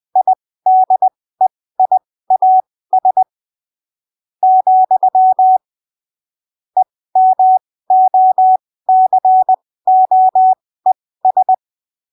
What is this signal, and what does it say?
Morse code, heard as a keyed tone, about 20 wpm, spaced as for 15 wpm, 750 Hz. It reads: IDEIAS , EMOCOES
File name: arquivos / texto / texto 18 20ppm